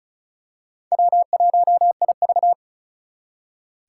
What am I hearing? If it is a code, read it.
Morse code, heard as a keyed tone, about 36 wpm, 700 Hz. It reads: W1IV
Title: Four Challenging Letter-Number Combinations 35wpm